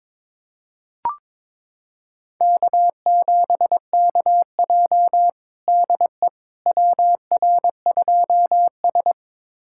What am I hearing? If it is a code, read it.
Morse code, heard as a keyed tone, about 21 wpm, 700 Hz. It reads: K7KJ DE WR2S
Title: USA Callsign DE Callsign - 4 Characters - 22wpm